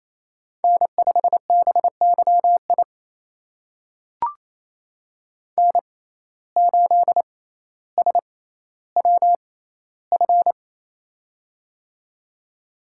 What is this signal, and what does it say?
Morse code, heard as a keyed tone, about 28 wpm, spaced as for 14 wpm, 700 Hz. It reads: N5BYI N8SWF
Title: USA Call Signs - 5 Characters - 6x Character Spacing - 28wpm